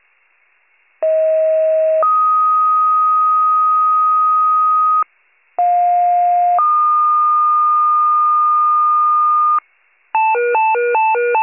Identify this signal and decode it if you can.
Two-tone page, heard as a one-second tone, then a three-second tone, about 640 then 1250 Hz, about 710 then 1220 Hz. It is heard from 1.0 to 9.6 s.
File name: pager / 83 RESCUE OFC / wcalert